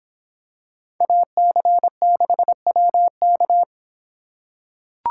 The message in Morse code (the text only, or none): AC6WK